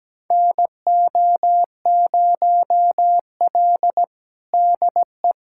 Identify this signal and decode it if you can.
Morse code, heard as a keyed tone, about 17 wpm, 700 Hz. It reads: NO0L DE
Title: USA Callsign DE Callsign - 4 Characters - 17wpm